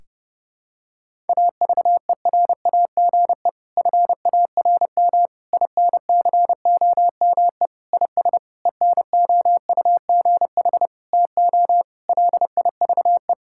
Morse code, heard as a keyed tone, about 32 wpm, 700 Hz. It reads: AVERAGE FARM INCOME IS ENOUGH TO LIVE
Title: Sentences from Top 1000 Words - Review Part 6 - 30wpm